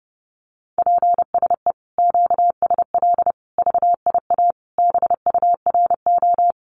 Morse code, heard as a keyed tone, about 30 wpm, 700 Hz. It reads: PSE QSL VIA BURO
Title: Common QSO Phrases Encore 30wpm